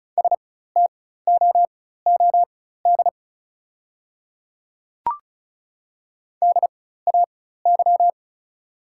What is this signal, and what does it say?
Morse code, heard as a keyed tone, about 36 wpm, spaced as for 16 wpm, 690 Hz. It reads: STOOD DAY
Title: 4x Character Spacing - Top 500 Words - 35wpm